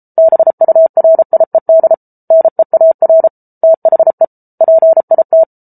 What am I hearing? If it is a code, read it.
Morse code, heard as a keyed tone, about 34 wpm, 650 Hz. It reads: BURIED NEAR THE PIT